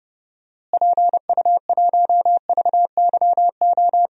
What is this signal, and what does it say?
Morse code, heard as a keyed tone, about 30 wpm, 700 Hz. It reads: PU1VYO
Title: DXCC-100 Call Signs - 6 Characters - Repeated 2x - 30wpm